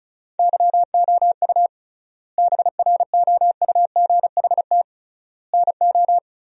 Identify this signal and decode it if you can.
Morse code, heard as a keyed tone, about 34 wpm, 700 Hz. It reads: YOU BROUGHT NO
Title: 3x Word-Spacing - Sentences from Top 500 Words - 35wpm